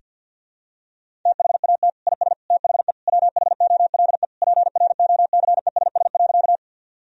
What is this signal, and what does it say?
Morse code, heard as a keyed tone, about 49 wpm, 700 Hz. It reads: THAT IS THE WHOLE PROCESS.
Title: Sentences from Top 1000 Words - Review Part 3 - 50wpm